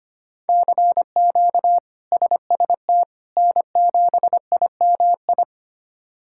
Morse code, heard as a keyed tone, about 25 wpm, 700 Hz. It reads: CQ SST N7IMI
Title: K1USN SST QSO Practice - 25wpm